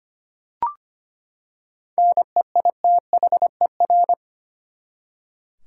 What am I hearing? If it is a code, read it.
Morse code, heard as a keyed tone, about 25 wpm, 700 Hz. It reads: NEITHER
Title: Seven Letter Words - 25wpm